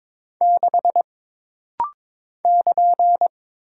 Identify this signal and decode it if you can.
Morse code, heard as a keyed tone, about 22 wpm, 700 Hz. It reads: 6 <KN>
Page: This rapid-fire timing is geared to push your ICR (Instant Character Recognition) skills.